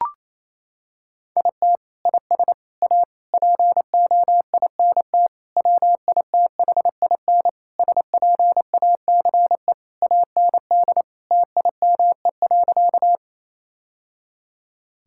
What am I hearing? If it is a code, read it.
Morse code, heard as a keyed tone, about 28 wpm, 700 Hz. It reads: IT IS A POINT WITHIN SPACE AND TIME.
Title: Sentences from Top 1000 Words - Review Part 5 - 28wpm